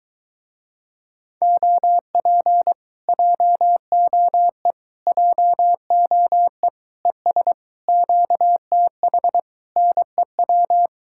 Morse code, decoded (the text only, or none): OP JOE JOE ES QTH NEW